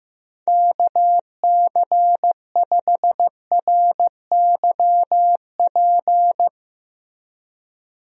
Morse code, heard as a keyed tone, about 16 wpm, 700 Hz. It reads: KC5RYP
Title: USA Call Signs - 6 Characters - Repeated 2x - 15wpm